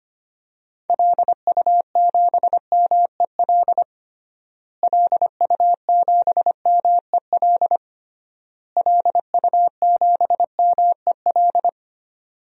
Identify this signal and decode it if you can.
Morse code, heard as a keyed tone, about 25 wpm, 700 Hz. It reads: LU7MEL LU7MEL LU7MEL
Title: DXCC-100 Call Signs - 6 Characters - Repeated 3x - 25wpm